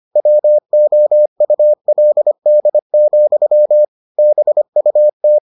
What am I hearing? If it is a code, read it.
Morse code, heard as a keyed tone, about 25 wpm, 600 Hz. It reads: WOULD, BUT